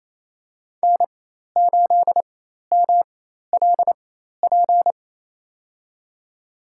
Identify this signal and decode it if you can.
Morse code, heard as a keyed tone, about 27 wpm, spaced as for 13 wpm, 700 Hz. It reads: N8MLP
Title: USA Call Signs - 5 Characters - 4x Character Spacing - 28wpm